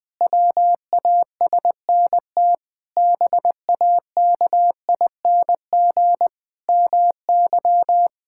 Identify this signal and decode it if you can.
Morse code, heard as a keyed tone, about 20 wpm, 700 Hz. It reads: WASNT BAKING MY
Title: Pangram Challenge v2 - 20wpm